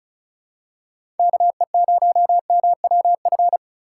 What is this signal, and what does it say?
Morse code, heard as a keyed tone, about 34 wpm, 700 Hz. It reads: KE0MWF
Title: USA Call Signs - 6 Characters - 2x Character Spacing - 35wpm